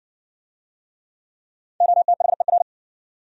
Morse code, heard as a keyed tone, about 58 wpm, 700 Hz. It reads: OTHER